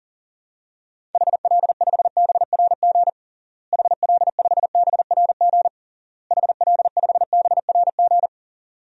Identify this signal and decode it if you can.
Morse code, heard as a keyed tone, about 37 wpm, 700 Hz. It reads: HL5BRG HL5BRG HL5BRG
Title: DXCC-100 Call Signs - 6 Characters - Repeated 3x - 40wpm